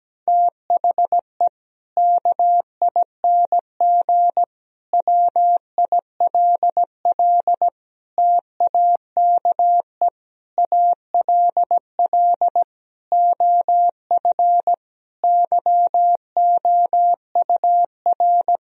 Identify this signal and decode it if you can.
Morse code, heard as a keyed tone, about 17 wpm, 700 Hz. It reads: THE KING WILL TAKE ALL OF YOUR
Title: Sentences from Top 1000 Words - Review Part 4 - 17wpm